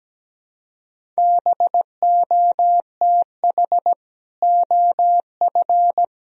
Morse code, heard as a keyed tone, about 17 wpm, 700 Hz. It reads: BOTH OF
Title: Sentences from Top 1000 Words - Review Part 1 - 17wpm